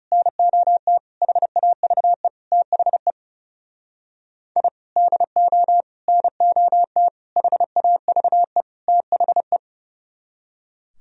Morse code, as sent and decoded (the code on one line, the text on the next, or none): -. --- - / .... .- ...- . / - .... / .. / -.. --- / -. --- - / .... .- ...- . / - .... .
NOT HAVE THE I DO NOT HAVE THE